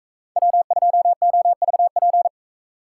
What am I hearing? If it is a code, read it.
Morse code, heard as a keyed tone, about 44 wpm, 700 Hz. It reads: W2OVP